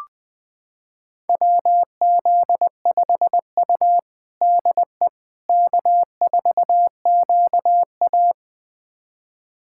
Morse code, heard as a keyed tone, about 20 wpm, 700 Hz. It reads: WZ5U DE K4QA
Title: USA Callsign DE Callsign - 4 Characters - 20wpm